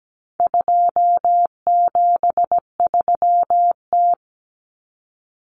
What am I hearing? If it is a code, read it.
Morse code, heard as a keyed tone, about 17 wpm, 700 Hz. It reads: 273T
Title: Common Cut Numbers - 4-Digit 17wpm